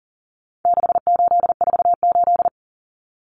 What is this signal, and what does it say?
Morse code, heard as a keyed tone, about 40 wpm, 700 Hz. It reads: 6848
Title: Numbers - 4-Digit 40wpm